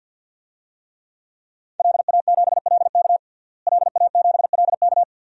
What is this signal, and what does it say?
Morse code, heard as a keyed tone, about 50 wpm, 700 Hz. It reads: LA7LK LA7LK
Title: DXCC-100 Call Signs - 5 Characters - Repeated 2x - 50wpm